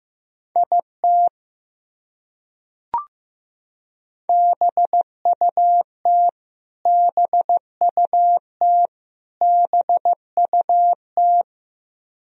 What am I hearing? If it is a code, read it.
Morse code, heard as a keyed tone, about 15 wpm, 700 Hz. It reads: IT BUT BUT BUT
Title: Top 100 Words Repeated 15wpm